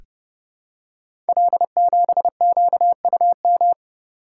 Morse code, heard as a keyed tone, about 28 wpm, 700 Hz. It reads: L7QUM
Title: Code Groups - 5 Letter:Numbers - 30wpm